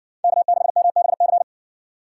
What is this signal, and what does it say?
Morse code, heard as a keyed tone, about 55 wpm, 700 Hz. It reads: K6MBC